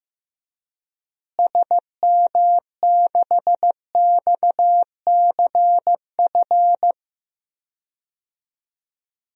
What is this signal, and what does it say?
Morse code, heard as a keyed tone, about 15 wpm, 700 Hz. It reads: SM6XCF
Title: DXCC-100 Call Signs - 6 Characters - Repeated 3x - 15wpm